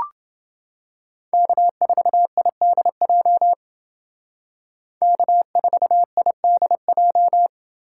Morse code, heard as a keyed tone, about 28 wpm, 700 Hz. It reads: K4IDJ K4IDJ